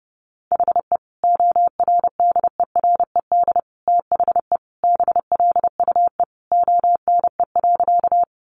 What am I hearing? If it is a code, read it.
Morse code, heard as a keyed tone, about 30 wpm, 700 Hz. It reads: HE ORDERED THE BLUE ONE.
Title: Additional Easy Phrases 30wpm